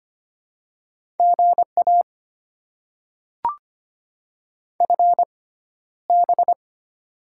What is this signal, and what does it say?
Morse code, heard as a keyed tone, about 25 wpm, spaced as for 13 wpm, 690 Hz. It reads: GA FB
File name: Top 20 QSO Elements - 6x Character Spacing - 25wpm